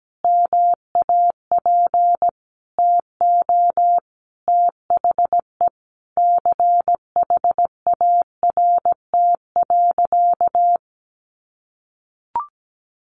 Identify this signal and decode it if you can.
Morse code, heard as a keyed tone, about 17 wpm, 700 Hz. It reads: MAP TO THE CHART.
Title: Sentences from Top 1300 Words 17wpm